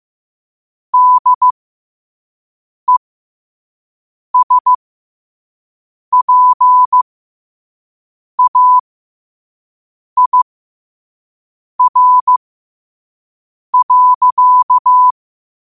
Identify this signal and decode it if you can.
Morse code, heard as a keyed tone, about 15 wpm, spaced as for 5 wpm, 1000 Hz. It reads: DESPAIR.
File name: qotdfiles / 5wpm